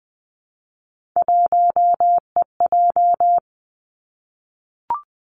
Morse code, heard as a keyed tone, about 20 wpm, 700 Hz. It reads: 1EJ
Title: Three Challenging Letter-Number Combinations 20wpm